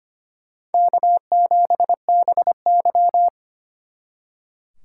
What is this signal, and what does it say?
Morse code, heard as a keyed tone, about 32 wpm, 700 Hz. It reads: K7BY